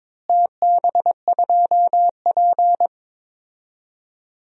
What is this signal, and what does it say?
Morse code, heard as a keyed tone, about 22 wpm, 700 Hz. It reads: TB2P
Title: DXCC-100 Call Signs - 4 Characters - Repeated 2x - 22wpm